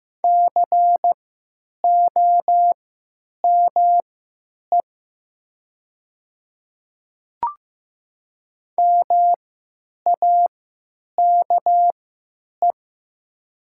Morse code, heard as a keyed tone, about 15 wpm, spaced as for 9 wpm, 700 Hz. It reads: COME MAKE
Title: 3x Character Spacing - Top 100 Words - 15wpm